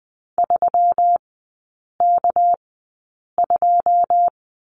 Morse code, heard as a keyed tone, about 20 wpm, spaced as for 8 wpm, 700 Hz. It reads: 3K2
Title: 198 Lesson 32e - Letter k - Three Characters 20wpm